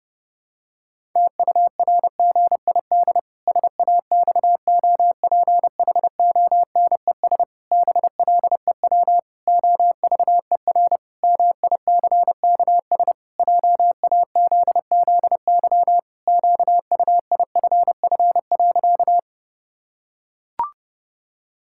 Morse code, heard as a keyed tone, about 32 wpm, 700 Hz. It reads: TURGID SAXOPHONES BLEW OVER MICKS JAZZY QUIFF.
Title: Pangram Challenge v2 - 30wpm